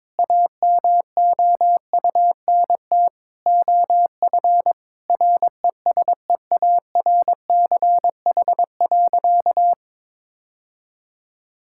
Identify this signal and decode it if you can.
Morse code, heard as a keyed tone, about 22 wpm, 700 Hz. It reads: AMOUNT OF RESEARCH.
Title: Sentences from Top 1000 Words - Review Part 5 - 22wpm